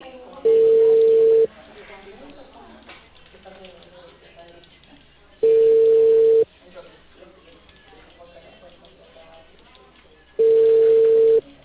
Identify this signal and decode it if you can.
Ringback tone (China, 450 Hz), with speech.